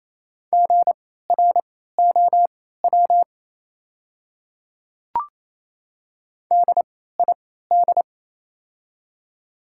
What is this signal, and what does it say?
Morse code, heard as a keyed tone, about 28 wpm, spaced as for 16 wpm, 700 Hz. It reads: GROW DID
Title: 3x Character Spacing - Top 300 Words - 28wpm